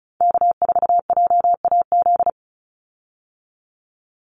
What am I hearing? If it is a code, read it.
Morse code, heard as a keyed tone, about 34 wpm, 700 Hz. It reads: K4JAZ
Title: Call Signs Repeated 3x - 35wpm